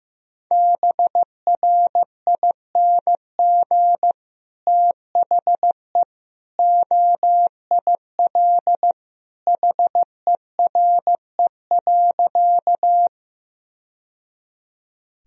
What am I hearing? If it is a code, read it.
Morse code, heard as a keyed tone, about 15 wpm, 700 Hz. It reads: BRING THE OIL HERE.
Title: Sentences from Top 1000 Words - Review Part 4 - 15wpm